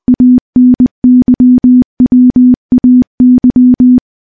Morse code, heard as a keyed tone, about 20 wpm, 250 Hz. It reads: ANYWAY
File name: morse-training-20WPM